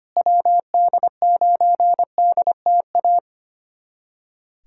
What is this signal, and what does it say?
Morse code, heard as a keyed tone, about 25 wpm, 700 Hz. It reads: WD9DTA
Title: USA Callsign DE Callsign - 6 Characters - 25wpm